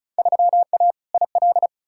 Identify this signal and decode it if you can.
Morse code, heard as a keyed tone, about 36 wpm, 700 Hz. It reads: 3A IL
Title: Field Day Exchanges - 35wpm